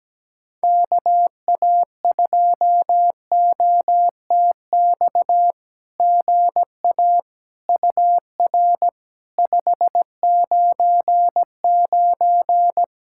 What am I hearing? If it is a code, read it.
Morse code, heard as a keyed tone, about 17 wpm, 700 Hz. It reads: KA2OTX GA UR 599
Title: SOTA QSO Practice 17wpm